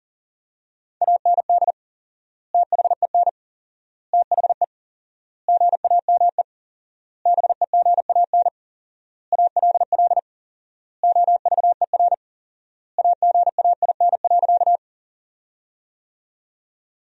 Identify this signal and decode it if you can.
Morse code, heard as a keyed tone, about 40 wpm, 700 Hz. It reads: AND THEN THE GAME BEGAN ALL OVER AGAIN.
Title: 4x Word-Spacing - Sentences from Top 500 Words - 40wpm